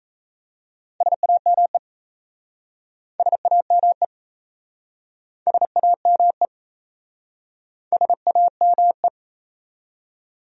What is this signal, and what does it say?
Morse code, heard as a keyed tone, about 37 wpm, 690 Hz. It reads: SAME SAME SAME SAME